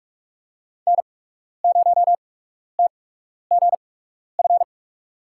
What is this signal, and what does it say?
Morse code, heard as a keyed tone, about 42 wpm, spaced as for 12 wpm, 700 Hz. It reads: N0TGF
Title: USA Call Signs - 5 Characters - 8x Character Spacing - 45wpm